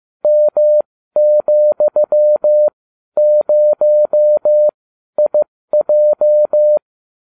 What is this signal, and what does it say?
Morse code, heard as a keyed tone, about 15 wpm, spaced as for 12 wpm, 600 Hz. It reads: M,0IJ